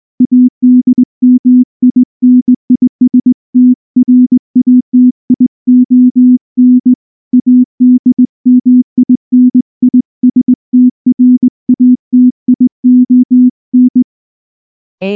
Morse code, heard as a keyed tone, about 19 wpm, 250 Hz. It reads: ADMINISTRATION ADMINISTRATION
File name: morse-training-20WPM